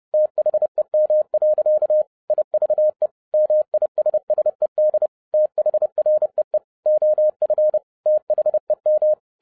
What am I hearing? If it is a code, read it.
Morse code, heard as a keyed tone, about 30 wpm, 610 Hz. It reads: THEM. IVE MISSED THREE OF THEM
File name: 2012-01-mp3 / n11-I DIDNT MISS THE SOLAR CYCLE 24-30wpm